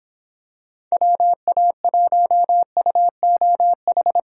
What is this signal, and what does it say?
Morse code, heard as a keyed tone, about 26 wpm, 700 Hz. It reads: WA1UOH